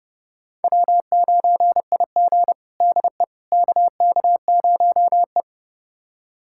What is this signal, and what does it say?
Morse code, heard as a keyed tone, about 30 wpm, 700 Hz. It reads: W9IG DE KK0E
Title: USA Callsign DE Callsign - 4 Characters - 30wpm